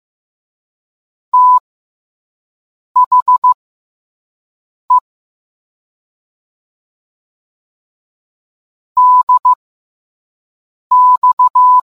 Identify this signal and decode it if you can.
Morse code, heard as a keyed tone, about 15 wpm, spaced as for 5 wpm, 1000 Hz. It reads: THE DX